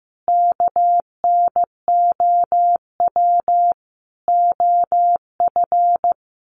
Morse code, heard as a keyed tone, about 15 wpm, 700 Hz. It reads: KNOW OF
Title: Sets of 5 Words Encore2 15wpm